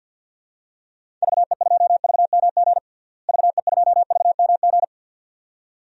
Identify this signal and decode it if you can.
Morse code, heard as a keyed tone, about 53 wpm, 700 Hz. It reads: VE2VMG VE2VMG
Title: DXCC-100 Call Signs - 6 Characters - Repeated 2x - 50wpm